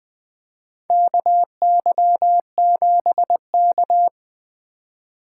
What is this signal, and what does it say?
Morse code, heard as a keyed tone, about 20 wpm, 700 Hz. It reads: KY7K
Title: K1USN SST QSO Practice - 20wpm